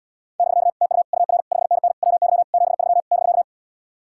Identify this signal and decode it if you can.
Morse code, heard as a keyed tone, about 77 wpm, 700 Hz. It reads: <BK> TU UR 5NN WY 73 <BK>